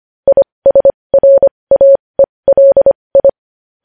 Morse code, heard as a keyed tone, about 25 wpm, spaced as for 20 wpm, 550 Hz. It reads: ISRAELI
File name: podcast / News headlines-1x-25x20